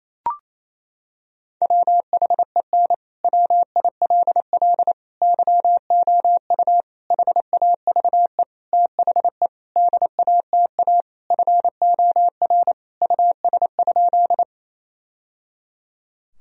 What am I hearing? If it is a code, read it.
Morse code, heard as a keyed tone, about 28 wpm, 700 Hz. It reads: WHEN WILL YOU HAVE THE DATA FOR US?
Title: Sentences from Top 1000 Words - Review Part 3 - 28wpm